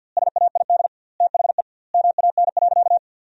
Morse code, heard as a keyed tone, about 50 wpm, 700 Hz. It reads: SAID THE MAN.
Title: 2x Word-Spacing - Sentences from Top 400 Words - 50wpm